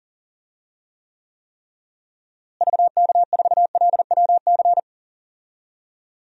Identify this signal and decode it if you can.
Morse code, heard as a keyed tone, about 40 wpm, 700 Hz. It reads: VK4LWC
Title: DXCC-100 Call Signs - 6 Characters - Repeated 3x - 40wpm